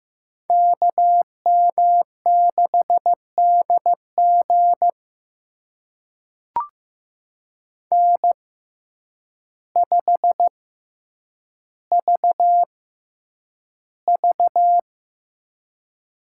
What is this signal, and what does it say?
Morse code, heard as a keyed tone, about 15 wpm, spaced as for 8 wpm, 700 Hz. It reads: KM6DG N5VV
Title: USA Call Signs - 5 Characters - 6x Character Spacing - 15wpm